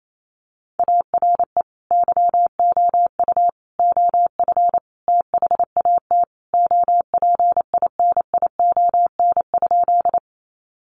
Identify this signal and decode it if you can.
Morse code, heard as a keyed tone, about 28 wpm, 700 Hz. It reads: ARE YOU OF THAT OPINION?
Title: Sentences from Top 1200 Words 28wpm